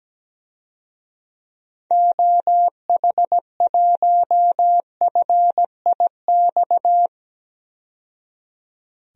Morse code, heard as a keyed tone, about 17 wpm, 700 Hz. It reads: OH1FIX
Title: DXCC-100 Call Signs - 6 Characters - 17wpm